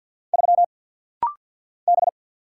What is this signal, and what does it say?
Morse code, heard as a keyed tone, about 50 wpm, 690 Hz. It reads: 3 B
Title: Single Letter-Number Alt - Rapid-Fire 50wpm